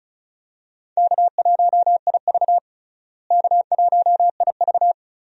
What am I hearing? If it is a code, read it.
Morse code, heard as a keyed tone, about 36 wpm, 700 Hz. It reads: K1IV K1IV
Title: USA Call Signs - 4 Characters - Repeated 3x - 35wpm